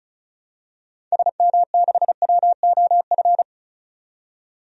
Morse code, heard as a keyed tone, about 34 wpm, 700 Hz. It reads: SM6WOF
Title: DXCC-100 Call Signs - 6 Characters - 35wpm